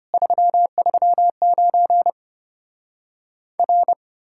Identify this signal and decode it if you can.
Morse code, heard as a keyed tone, about 33 wpm, 700 Hz. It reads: 339 R